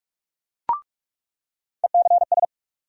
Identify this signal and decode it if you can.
Morse code, heard as a keyed tone, about 44 wpm, 700 Hz. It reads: ECS
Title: USA Call Sign Postfixes - 3 Letters 45wpm